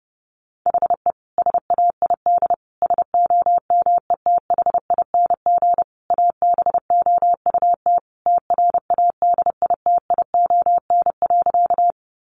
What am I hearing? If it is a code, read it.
Morse code, heard as a keyed tone, about 30 wpm, 700 Hz. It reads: HE SAID SOMETHING ABOUT TRADITION.
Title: Sentences from Top 1500 Words 30wpm